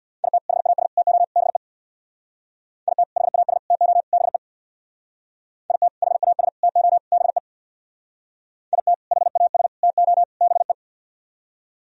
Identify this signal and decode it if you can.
Morse code, heard as a keyed tone, about 64 wpm, 700 Hz. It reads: IT HAS TO BE IT HAS TO BE IT HAS TO BE IT HAS TO BE